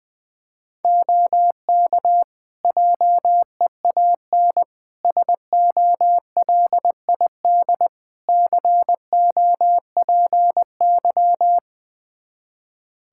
Morse code, heard as a keyed tone, about 21 wpm, 700 Hz. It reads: OK JEAN SOLID COPY
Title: Practice QSO - Exchange 3 - More Info - 20wpm